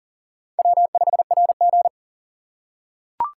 Morse code, heard as a keyed tone, about 40 wpm, 700 Hz. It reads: W5RG
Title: USA Call Signs - 4 Characters - 3x Character Spacing - 40wpm